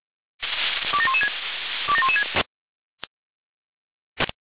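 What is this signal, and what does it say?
Five-tone selective call (ZVEI), heard as quick runs of five tones, tones of about 70 ms. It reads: 27116 27116
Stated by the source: Five-tone ZVEI selective call received by SDR.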